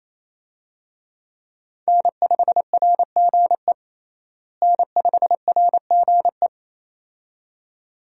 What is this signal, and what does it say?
Morse code, heard as a keyed tone, about 28 wpm, 700 Hz. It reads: N5RGE N5RGE
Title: USA Call Signs - 5 Characters - Repeated 2x - 28wpm